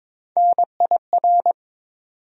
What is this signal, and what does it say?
Morse code, heard as a keyed tone, about 22 wpm, 700 Hz. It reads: NIR
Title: USA Call Sign Postfixes - 3 Letters 22wpm